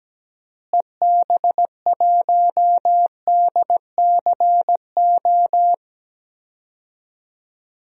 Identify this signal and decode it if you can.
Morse code, heard as a keyed tone, about 17 wpm, 700 Hz. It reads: EB1DCO
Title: DXCC-100 Call Signs - 6 Characters - Repeated 3x - 17wpm